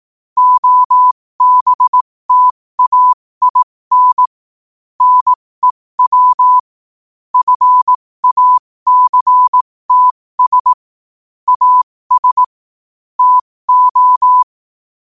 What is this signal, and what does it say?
Morse code, heard as a keyed tone, about 18 wpm, spaced as for 15 wpm, 1000 Hz. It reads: OBTAIN NEW FACTS AS TO
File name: qotdfiles / 15wpm